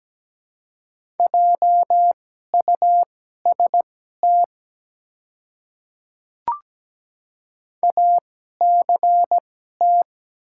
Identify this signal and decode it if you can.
Morse code, heard as a keyed tone, about 17 wpm, spaced as for 12 wpm, 700 Hz. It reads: JUST ACT
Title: 2x Character Spacing - Top 200 Words - 17wpm